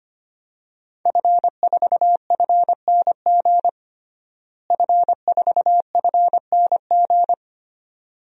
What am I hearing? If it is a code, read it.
Morse code, heard as a keyed tone, about 25 wpm, 700 Hz. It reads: F4FNG F4FNG
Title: DXCC-100 Call Signs - 5 Characters - Repeated 2x - 25wpm